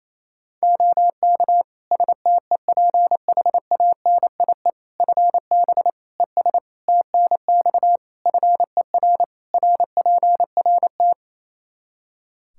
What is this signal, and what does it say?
Morse code, heard as a keyed tone, about 28 wpm, 700 Hz. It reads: OK STEPHANIE FB ES TNX FER RPRT
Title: Practice QSO - Exchange 2 - Info - 28wpm